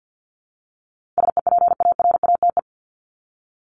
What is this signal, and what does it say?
Morse code, heard as a keyed tone, about 48 wpm, 700 Hz. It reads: SEPARATE
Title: Top 751-1000 Words 50wpm